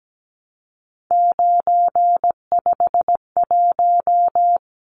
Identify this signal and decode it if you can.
Morse code, heard as a keyed tone, about 17 wpm, 700 Hz. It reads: 951
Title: Numbers - 3-Digit 17wpm